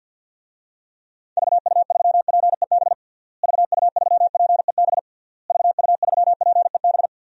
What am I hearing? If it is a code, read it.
Morse code, heard as a keyed tone, about 51 wpm, 700 Hz. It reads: VU3PEB VU3PEB VU3PEB
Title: DXCC-100 Call Signs - 6 Characters - Repeated 3x - 50wpm